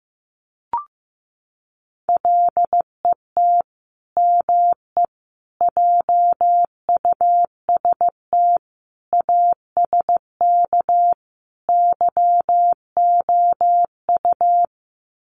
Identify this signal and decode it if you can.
Morse code, heard as a keyed tone, about 15 wpm, 700 Hz. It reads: LET ME JUST ASK YOU
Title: Sets of 5 Words Encore2 15wpm